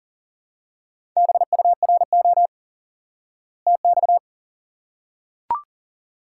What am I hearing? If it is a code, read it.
Morse code, heard as a keyed tone, about 40 wpm, 700 Hz. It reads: BURO TX
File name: IQE Course 3 - Lesson 89 - YAGI - Two Elements - 40wpm